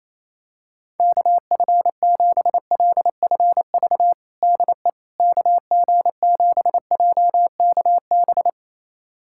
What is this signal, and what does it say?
Morse code, heard as a keyed tone, about 28 wpm, 700 Hz. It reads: KF7LFV DE KG7JKB
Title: USA Callsign DE Callsign 28wpm